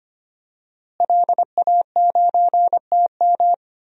Morse code, heard as a keyed tone, about 25 wpm, 700 Hz. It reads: LA9TM
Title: DXCC-100 Call Signs - 5 Characters - Repeated 2x - 25wpm